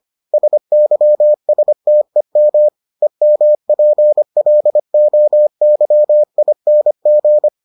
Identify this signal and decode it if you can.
Morse code, heard as a keyed tone, about 25 wpm, 600 Hz. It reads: SYSTEM EMPLOYING